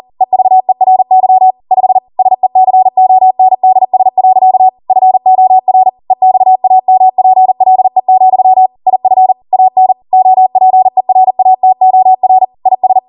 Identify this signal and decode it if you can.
Morse code, heard as a keyed tone, about 40 wpm, 750 Hz. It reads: EVERY 5 SECONDS. FOR EXAMPLE, IF AN OPERATOR IS
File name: Morse / 40WPM